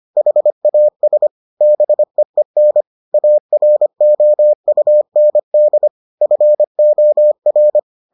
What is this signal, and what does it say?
Morse code, heard as a keyed tone, about 25 wpm, 600 Hz. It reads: HAS BEEN AROUND FOR